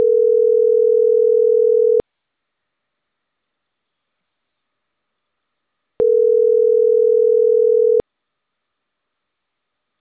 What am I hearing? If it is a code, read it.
Ringback tone (North America), 440 + 480 Hz.